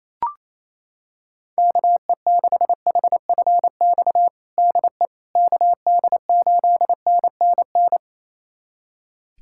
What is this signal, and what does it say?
Morse code, heard as a keyed tone, about 28 wpm, 700 Hz. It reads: KE6HFX DE KD8NNN
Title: USA Callsign DE Callsign - 6 Characters - 28wpm